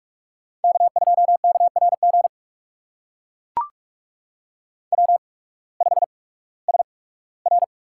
Morse code, heard as a keyed tone, about 44 wpm, spaced as for 21 wpm, 700 Hz. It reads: K2KRG W5SR
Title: USA Call Signs - 5 Characters - 8x Character Spacing - 45wpm